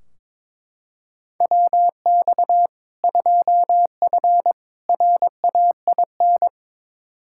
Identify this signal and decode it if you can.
Morse code, heard as a keyed tone, about 22 wpm, 700 Hz. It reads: WX 2F RAIN
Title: K1USN SST QSO Practice 22wpm